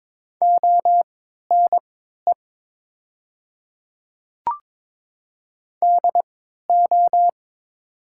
Morse code, heard as a keyed tone, about 22 wpm, spaced as for 12 wpm, 700 Hz. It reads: ONE DO
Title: 3x Character Spacing - Top 200 Words - 22wpm